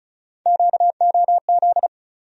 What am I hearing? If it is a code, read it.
Morse code, heard as a keyed tone, about 36 wpm, 700 Hz. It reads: QOZ
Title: USA Call Sign Postfixes - 3 Letters 35wpm